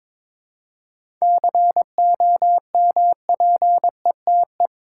Morse code, heard as a keyed tone, about 22 wpm, 700 Hz. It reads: COMPETE
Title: Seven Letter Words - 22wpm